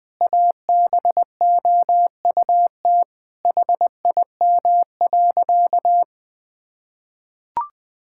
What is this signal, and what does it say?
Morse code, heard as a keyed tone, about 20 wpm, 700 Hz. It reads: ABOUT HIM.
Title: Sentences from Top 1000 Words - Review Part 5 - 20wpm